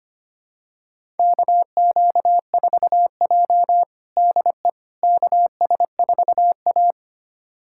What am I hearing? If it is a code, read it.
Morse code, heard as a keyed tone, about 25 wpm, 700 Hz. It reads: KQ4J DE KS4A
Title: USA Callsign DE Callsign - 4 Characters - 25wpm